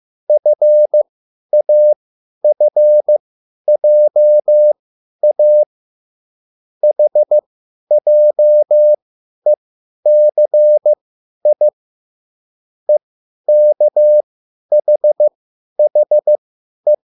Morse code, heard as a keyed tone, about 15 wpm, spaced as for 10 wpm, 600 Hz. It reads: FAFJA HJECI EKHHE
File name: QSL / CW lesson